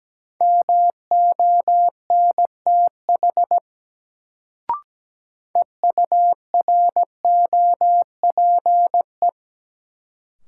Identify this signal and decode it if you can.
Morse code, heard as a keyed tone, about 17 wpm, 700 Hz. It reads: MONTH EUROPE
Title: Sets of 5 Words Encore - New Word Drill - 17wpm